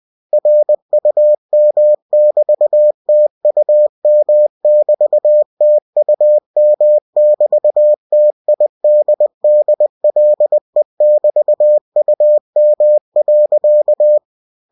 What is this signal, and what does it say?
Morse code, heard as a keyed tone, about 20 wpm, 600 Hz. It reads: RUM<BT>TUM<BT>TUM<BT>TIDDLE<BT>UM.
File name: cw-books / winnie-the-pooh / winnie-20wpm-page-0013